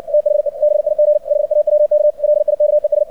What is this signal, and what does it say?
Morse code, heard as a keyed tone, about 60 wpm, 600 Hz. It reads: THE BIG HAWK LIKES
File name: archived / CRSnet